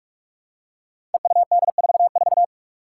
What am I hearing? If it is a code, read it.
Morse code, heard as a keyed tone, about 41 wpm, 700 Hz. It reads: EUD44
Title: Five Challenging Letter-Number Combinations 45wpm